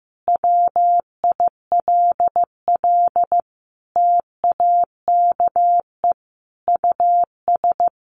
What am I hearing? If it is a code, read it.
Morse code, heard as a keyed tone, about 15 wpm, 700 Hz. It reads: WILL TAKE US
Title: Sentences from Top 1500 Words 15wpm